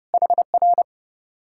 Morse code, heard as a keyed tone, about 39 wpm, 700 Hz. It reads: HR